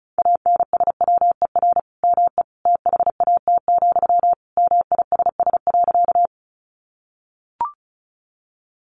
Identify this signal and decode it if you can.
Morse code, heard as a keyed tone, about 34 wpm, 700 Hz. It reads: ANSWER ME THAT, MISS.
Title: Sentences from Top 500 Words 35wpm